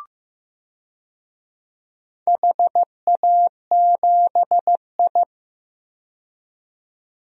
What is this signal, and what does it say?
Morse code, heard as a keyed tone, about 15 wpm, 700 Hz. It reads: HA7I
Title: DXCC-100 Call Signs - 4 Characters - 15wpm